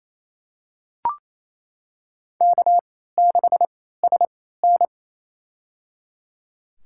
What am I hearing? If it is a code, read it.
Morse code, heard as a keyed tone, about 28 wpm, spaced as for 16 wpm, 700 Hz. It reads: K6SN
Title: USA Call Signs - 4 Characters - 3x Character Spacing - 28wpm